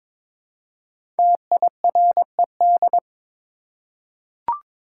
Morse code, heard as a keyed tone, about 22 wpm, 700 Hz. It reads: TIRED
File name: 5 Letter Words - First 3 Letters Given - 22wpm